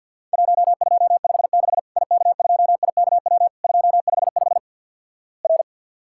Morse code, heard as a keyed tone, about 48 wpm, 700 Hz. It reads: 1256 IK2IKW 255 R
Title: Worked All Europe DX Contest - QTC Exchange with Repeats - 50wpm